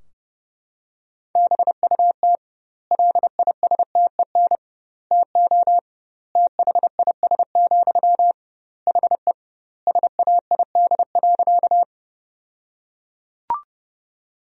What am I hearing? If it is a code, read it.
Morse code, heard as a keyed tone, about 30 wpm, 700 Hz. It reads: BUT LISTEN TO THIS, HE SAID.
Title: 2x Word-Spacing - Sentences from Top 400 Words - 30wpm